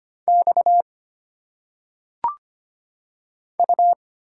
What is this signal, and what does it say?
Morse code, heard as a keyed tone, about 25 wpm, 700 Hz. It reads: X U
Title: Single Letters - Phonetic Pronunciation 25wpm